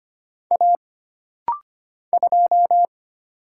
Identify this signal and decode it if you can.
Morse code, heard as a keyed tone, about 25 wpm, 700 Hz. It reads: A 2
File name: Single Character (Letter:Number:Punctuation:Prosign) - Rapid-Fire - 25wpm